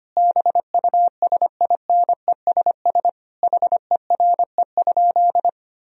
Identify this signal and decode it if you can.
Morse code, heard as a keyed tone, about 25 wpm, 700 Hz. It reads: BUSINESS HERE?
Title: Sentences from Top 1000 Words - Review Part 2 - 25wpm